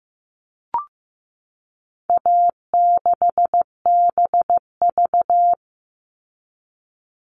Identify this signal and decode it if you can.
Morse code, heard as a keyed tone, about 15 wpm, 700 Hz. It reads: A6BV
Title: All Cut Numbers - 4-Digit 15wpm